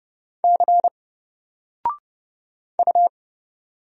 Morse code, heard as a keyed tone, about 30 wpm, 700 Hz. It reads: C U
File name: Single-Letter-Number-30wpm